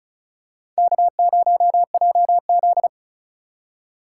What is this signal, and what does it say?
Morse code, heard as a keyed tone, about 36 wpm, 700 Hz. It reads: K0JZ
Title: USA Call Signs - 4 Characters - 4x Character Spacing - 35wpm